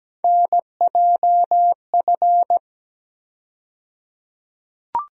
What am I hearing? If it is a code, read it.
Morse code, heard as a keyed tone, about 17 wpm, 700 Hz. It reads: NJF
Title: USA Call Sign Postfixes - 3 Letters 17wpm